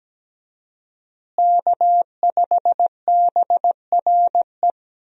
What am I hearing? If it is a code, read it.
Morse code, heard as a keyed tone, about 17 wpm, 700 Hz. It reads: K5BRE
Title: USA Call Signs - 5 Characters - Repeated 3x - 17wpm